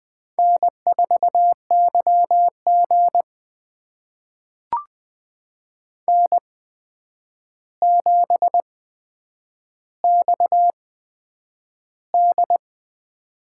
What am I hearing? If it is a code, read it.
Morse code, heard as a keyed tone, about 20 wpm, spaced as for 9 wpm, 700 Hz. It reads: N4YG N7XD
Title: USA Call Signs - 4 Characters - 8x Character Spacing - 20wpm